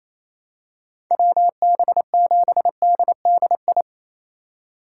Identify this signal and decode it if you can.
Morse code, heard as a keyed tone, about 27 wpm, 700 Hz. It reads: WB7DDI
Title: USA Call Signs - 6 Characters - 3x Character Spacing - 28wpm